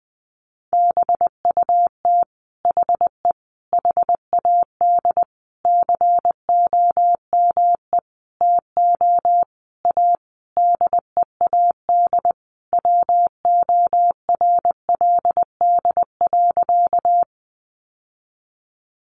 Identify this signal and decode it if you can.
Morse code, heard as a keyed tone, about 20 wpm, 700 Hz. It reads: BUT HE HAD COME TO A DEAD WORLD.
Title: Sentences from Top 1500 Words 20wpm